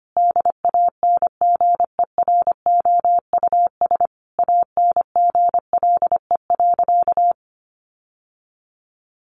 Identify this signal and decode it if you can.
Morse code, heard as a keyed tone, about 25 wpm, 700 Hz. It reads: DANGEROUS ANGLE.
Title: Sentences from Top 1500 Words 25wpm